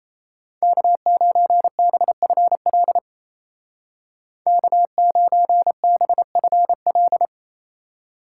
Morse code, heard as a keyed tone, about 36 wpm, 700 Hz. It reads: K9BFL K9BFL